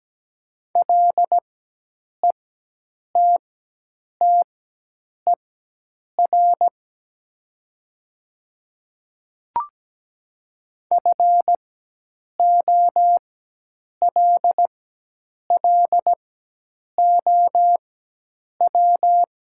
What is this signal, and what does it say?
Morse code, heard as a keyed tone, about 17 wpm, spaced as for 8 wpm, 700 Hz. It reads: LETTER FOLLOW
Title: 4x Character Spacing - Top 300 Words - 17wpm